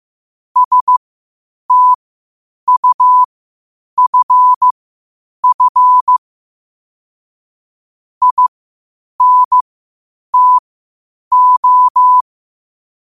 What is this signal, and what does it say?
Morse code, heard as a keyed tone, about 15 wpm, spaced as for 8 wpm, 1000 Hz. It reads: STUFF INTO
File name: qotdfiles / 8wpm